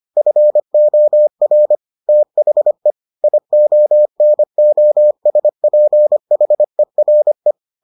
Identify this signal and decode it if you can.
Morse code, heard as a keyed tone, about 25 wpm, 600 Hz. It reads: FOR THE IONOSPHERE